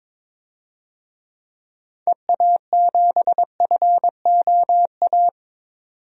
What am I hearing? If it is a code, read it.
Morse code, heard as a keyed tone, about 22 wpm, 700 Hz. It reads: EA7FOA
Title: DXCC-100 Call Signs - 6 Characters - 22wpm